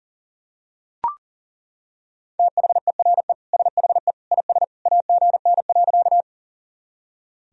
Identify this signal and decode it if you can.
Morse code, heard as a keyed tone, about 40 wpm, 700 Hz. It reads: THERE SHE IS AGN.
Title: Sentences from Top 200 Words and Abbreviations 40wpm